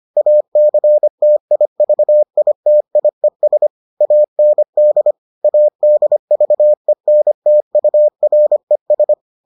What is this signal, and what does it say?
Morse code, heard as a keyed tone, about 25 wpm, 600 Hz. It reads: ACTIVITIES AND ADVENTURES